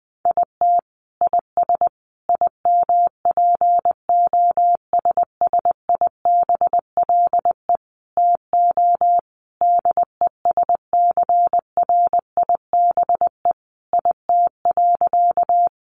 Morse code, heard as a keyed tone, about 20 wpm, 700 Hz. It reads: IT IS IMPOSSIBLE TO DESCRIBE IT.
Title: Sentences from Top 1100 Words 20wpm